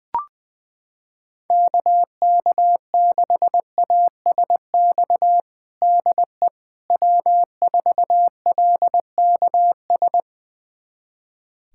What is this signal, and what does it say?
Morse code, heard as a keyed tone, about 20 wpm, 700 Hz. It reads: KK6ASX DE W4LKS
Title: USA Callsign DE Callsign 20wpm